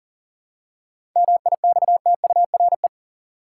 Morse code, heard as a keyed tone, about 40 wpm, 700 Hz. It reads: MIXTURE
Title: Seven Letter Words - 40wpm